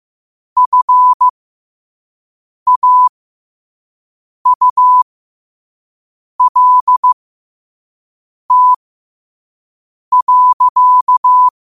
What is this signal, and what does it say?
Morse code, heard as a keyed tone, about 15 wpm, spaced as for 5 wpm, 1000 Hz. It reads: FAULT.
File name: qotdfiles / 5wpm